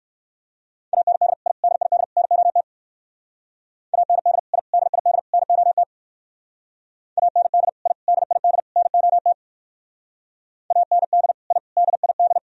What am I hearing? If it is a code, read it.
Morse code, heard as a keyed tone, about 58 wpm, 700 Hz. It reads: AND I DID NOT AND I DID NOT AND I DID NOT AND I DID